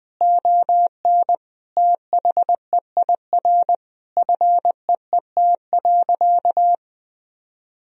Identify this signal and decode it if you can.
Morse code, heard as a keyed tone, about 21 wpm, 700 Hz. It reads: ON THEIR FEET.
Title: Sentences from Top 1000 Words - Review Part 1 - 20wpm